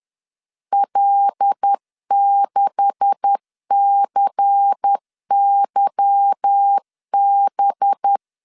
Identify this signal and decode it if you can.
Morse code, heard as a keyed tone, about 10 wpm, 800 Hz. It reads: L6CYB